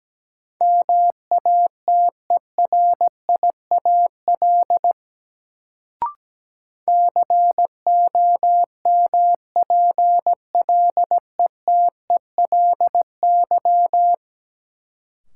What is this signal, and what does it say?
Morse code, heard as a keyed tone, about 17 wpm, 700 Hz. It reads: MATERIAL COMPLETELY
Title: Sentences from Top 1100 Words - New Word Drill Alt - 17wpm